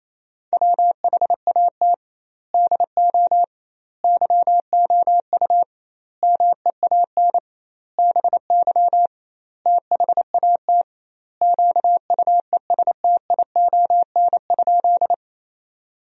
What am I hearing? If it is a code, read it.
Morse code, heard as a keyed tone, about 28 wpm, 700 Hz. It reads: WHAT DO YOU MEAN BY THAT QUESTION?
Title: 2x Word-Spacing - Sentences from Top 400 Words - 28wpm